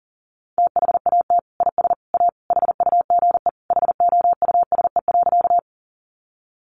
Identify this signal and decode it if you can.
Morse code, heard as a keyed tone, about 40 wpm, 700 Hz. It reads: THAT IS A HUGE HOUSE.
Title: Sentences from Top 1100 Words 40wpm